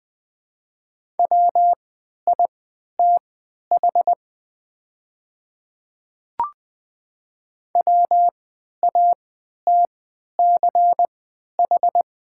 Morse code, heard as a keyed tone, about 20 wpm, spaced as for 11 wpm, 690 Hz. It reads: WITH WATCH
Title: 3x Character Spacing - Top 300 Words - 20wpm